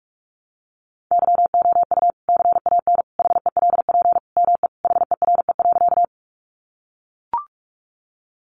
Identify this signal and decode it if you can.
Morse code, heard as a keyed tone, about 44 wpm, 700 Hz. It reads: YOU CAN HELP ME HERE.
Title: Sentences from Top 200 Words 45wpm